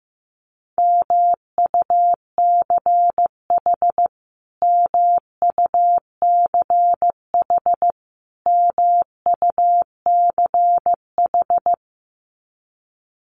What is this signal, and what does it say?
Morse code, heard as a keyed tone, about 15 wpm, 700 Hz. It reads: MUCH MUCH MUCH
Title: Top 200 Words Repeated 15wpm